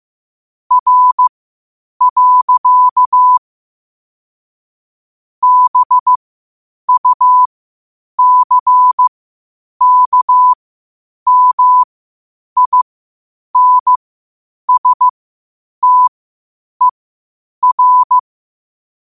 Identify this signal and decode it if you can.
Morse code, heard as a keyed tone, about 15 wpm, spaced as for 8 wpm, 1000 Hz. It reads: R. BUCKMINSTER
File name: qotdfiles / 8wpm